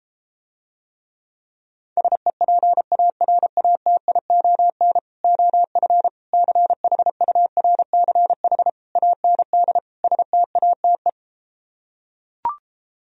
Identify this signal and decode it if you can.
Morse code, heard as a keyed tone, about 33 wpm, 700 Hz. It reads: SEPARATION OF CHURCH AND STATE